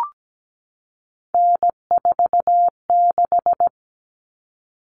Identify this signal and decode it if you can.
Morse code, heard as a keyed tone, about 17 wpm, 690 Hz. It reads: N46
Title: Common Cut Numbers - 3-Digit 17wpm